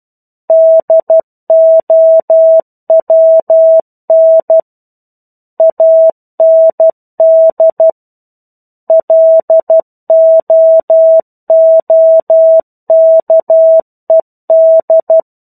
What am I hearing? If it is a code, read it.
Morse code, heard as a keyed tone, about 12 wpm, 650 Hz. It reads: DOWN AND LOOKED
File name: 01-Chapter-01 12 WPM